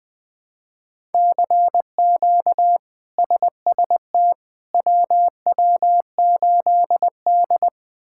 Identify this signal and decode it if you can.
Morse code, heard as a keyed tone, about 20 wpm, 700 Hz. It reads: CQ SST WW8D
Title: K1USN SST QSO Practice - 20wpm